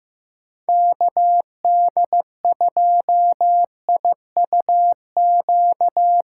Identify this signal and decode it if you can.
Morse code, heard as a keyed tone, about 17 wpm, 700 Hz. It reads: KD2IUQ
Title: Speed Racing - USA 2x3 Call Signs 15wpm